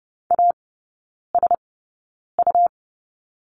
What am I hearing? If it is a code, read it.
Morse code, heard as a keyed tone, about 30 wpm, spaced as for 9 wpm, 690 Hz. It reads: ASU
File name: Lesson 17e - Letter u - Three Characters 30wpm